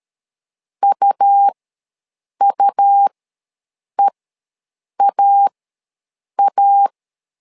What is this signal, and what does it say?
Morse code, heard as a keyed tone, about 13 wpm, spaced as for 7 wpm, 800 Hz. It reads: UUEAA
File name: LEZIONI / 2 UHDN / 2 U ws 12wpm 1655